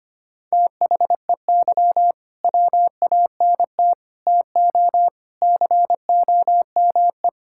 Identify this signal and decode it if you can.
Morse code, heard as a keyed tone, about 25 wpm, 700 Hz. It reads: THEY WANT TO COME
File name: Sentences with 3-Word Phrases - 25wpm